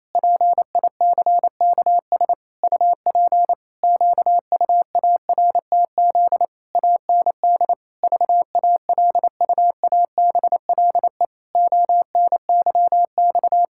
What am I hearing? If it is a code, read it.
Morse code, heard as a keyed tone, about 28 wpm, 700 Hz. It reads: PICKS UP QUARTZ AND VALUABLE ONYX
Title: Pangram Challenge v2 - 28wpm